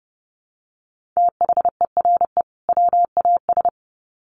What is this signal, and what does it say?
Morse code, heard as a keyed tone, about 30 wpm, 700 Hz. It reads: THERE WAS
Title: Sets of 2 Words 30wpm